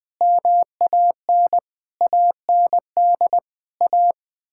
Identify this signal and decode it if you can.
Morse code, heard as a keyed tone, about 20 wpm, 700 Hz. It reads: MAN AND A
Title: Sentences with 6-Word Phrases - 20wpm